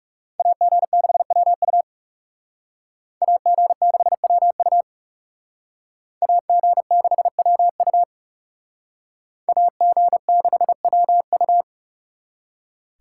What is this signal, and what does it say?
Morse code, heard as a keyed tone, about 37 wpm, 700 Hz. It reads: AG6WU AG6WU AG6WU AG6WU
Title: Speed Racing - USA Call Signs 30wpm